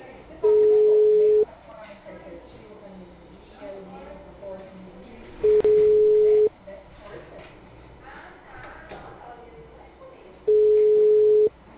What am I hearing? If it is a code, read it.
Ringback tone (Europe, 425 Hz), with speech.
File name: out- -2019- - - .2891